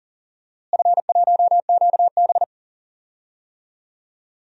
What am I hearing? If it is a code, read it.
Morse code, heard as a keyed tone, about 37 wpm, 700 Hz. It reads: F1QB
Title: DXCC-100 Call Signs - 4 Characters - Repeated 2x - 40wpm